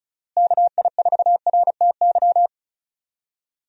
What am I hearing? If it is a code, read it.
Morse code, heard as a keyed tone, about 36 wpm, 700 Hz. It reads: KI4RTY
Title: USA Call Signs - 6 Characters - Repeated 2x - 35wpm